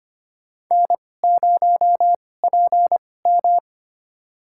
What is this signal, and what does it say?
Morse code, heard as a keyed tone, about 25 wpm, spaced as for 18 wpm, 700 Hz. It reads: N0PM
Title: USA Call Signs - 4 Characters - 2x Character Spacing - 25wpm